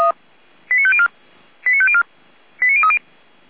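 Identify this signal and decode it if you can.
Five-tone selective call (ZVEI), heard as quick runs of five tones, tones of about 70 ms. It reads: 89504 89504 89030
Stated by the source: ZVEI five-tone call 89001 received by SDR, recorded by Feuerwehr Orlamünde.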